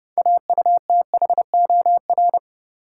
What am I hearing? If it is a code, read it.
Morse code, heard as a keyed tone, about 30 wpm, 700 Hz. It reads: AUTHOR
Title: Sets of 4 Words - New Word Drill - 30wpm